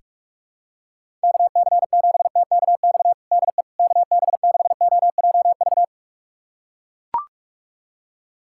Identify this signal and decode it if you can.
Morse code, heard as a keyed tone, about 41 wpm, 700 Hz. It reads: KC7TKX DE KB6OJV
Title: USA Callsign DE Callsign - 6 Characters - 45wpm